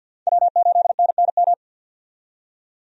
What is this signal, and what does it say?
Morse code, heard as a keyed tone, about 50 wpm, 700 Hz. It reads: W8NNM
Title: USA Call Signs - 5 Characters - 50wpm